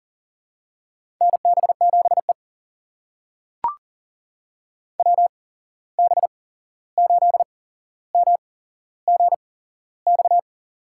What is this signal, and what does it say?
Morse code, heard as a keyed tone, about 42 wpm, spaced as for 15 wpm, 700 Hz. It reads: NB7E WB8MGX
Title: USA Call Signs - 8x Character Spacing - 40wpm